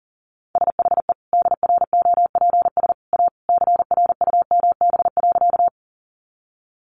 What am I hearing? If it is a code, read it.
Morse code, heard as a keyed tone, about 37 wpm, 700 Hz. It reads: SHE DROPS A CRUMB.
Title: Additional Easy Phrases 40wpm